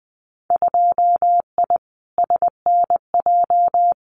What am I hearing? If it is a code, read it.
Morse code, heard as a keyed tone, about 20 wpm, 700 Hz. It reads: 2I SNJ
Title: Winter Field Day Exchanges Encore 20wpm